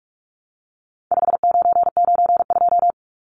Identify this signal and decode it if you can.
Morse code, heard as a keyed tone, about 44 wpm, 700 Hz. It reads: 5992
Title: Numbers - 4-Digit 45wpm